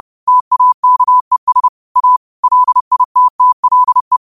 Morse code, heard as a keyed tone, about 30 wpm, 1000 Hz. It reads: TAKES A LITTLE